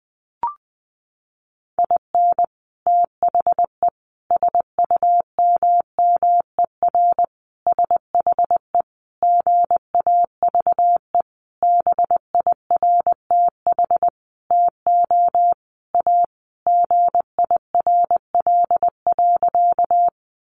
Morse code, heard as a keyed tone, about 20 wpm, 700 Hz. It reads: IN THE SUMMER SHE GAVE BIRTH TO A GIRL.
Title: Sentences from Top 1100 Words 20wpm